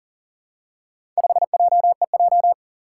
Morse code, heard as a keyed tone, about 40 wpm, 700 Hz. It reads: 5JEJ
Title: Four Challenging Letter-Number Combinations 40wpm